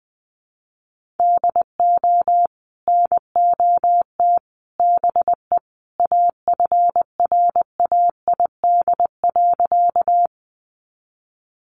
Morse code, heard as a keyed tone, about 20 wpm, 700 Hz. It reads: DO NOT BE AFRAID.
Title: Sentences from Top 1500 Words 20wpm